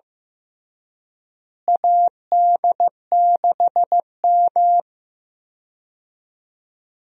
Morse code, heard as a keyed tone, about 15 wpm, 700 Hz. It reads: AD6M
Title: USA Call Signs - 4 Characters - Repeated 2x - 15wpm